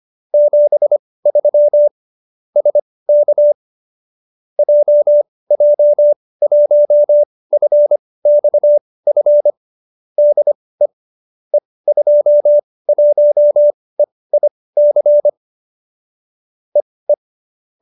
Morse code, heard as a keyed tone, about 25 wpm, spaced as for 18 wpm, 590 Hz. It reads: 73 SK JJ1FXF DE E21EIC EE